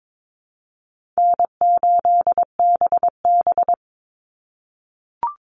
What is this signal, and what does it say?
Morse code, heard as a keyed tone, about 22 wpm, 700 Hz. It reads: N8BB
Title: NA SOTA Chaser Callsigns 22wpm